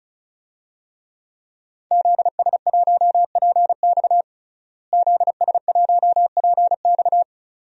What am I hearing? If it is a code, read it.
Morse code, heard as a keyed tone, about 34 wpm, 700 Hz. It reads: ZS1PX ZS1PX
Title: DXCC-100 Call Signs - 5 Characters - Repeated 2x - 35wpm